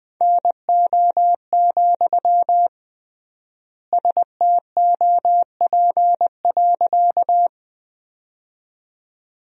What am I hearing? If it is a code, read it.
Morse code, heard as a keyed tone, about 21 wpm, 700 Hz. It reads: NO, STOP.
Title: 3x Word-Spacing - Sentences from Top 300 Words - 20wpm